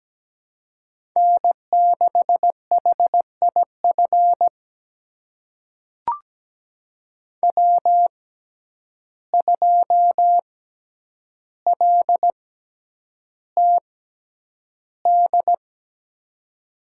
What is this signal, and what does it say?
Morse code, heard as a keyed tone, about 17 wpm, spaced as for 9 wpm, 700 Hz. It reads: N6HIF W2LTD
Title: USA Call Signs - 5 Characters - 6x Character Spacing - 17wpm